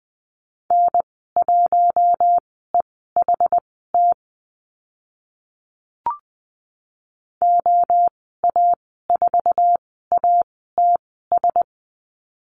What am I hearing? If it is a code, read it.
Morse code, heard as a keyed tone, about 20 wpm, spaced as for 14 wpm, 700 Hz. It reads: N1EHT OA4ATS
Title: 076 Lesson 13g - Letter L - Call Signs 20wpm